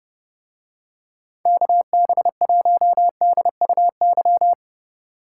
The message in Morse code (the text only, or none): KB1DUY